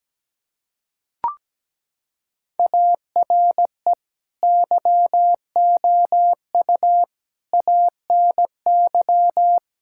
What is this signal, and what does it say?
Morse code, heard as a keyed tone, about 17 wpm, 700 Hz. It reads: ARE YOU ANY
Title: Sentences from Top 1000 Words - Review Part 6 - 17wpm